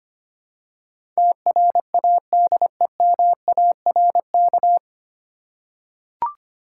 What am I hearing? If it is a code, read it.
Morse code, heard as a keyed tone, about 25 wpm, 690 Hz. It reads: TRADEMARK
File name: practice / 25wpm / 9 Letter Words - First 3 Letters Given - 25wpm